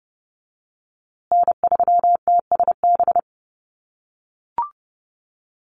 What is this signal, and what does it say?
Morse code, heard as a keyed tone, about 30 wpm, 700 Hz. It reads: N3TSB
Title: USA Call Signs Repeated 2x 30wpm